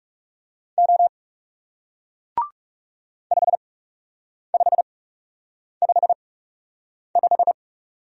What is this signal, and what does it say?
Morse code, heard as a keyed tone, about 39 wpm, spaced as for 8 wpm, 700 Hz. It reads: O 5555